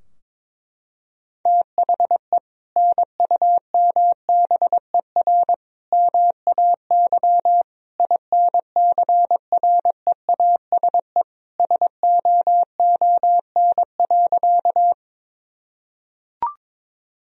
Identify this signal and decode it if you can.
Morse code, heard as a keyed tone, about 21 wpm, 700 Hz. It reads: THE NUMBER MAY INCREASE SOON.
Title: Sentences from Top 1000 Words - Review Part 4 - 22wpm